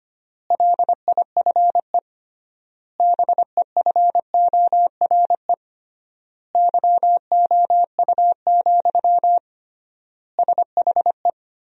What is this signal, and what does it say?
Morse code, heard as a keyed tone, about 25 wpm, 700 Hz. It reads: LIFE BEFORE YOU, SHE
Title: 3x Word-Spacing - Sentences from Top 400 Words - 25wpm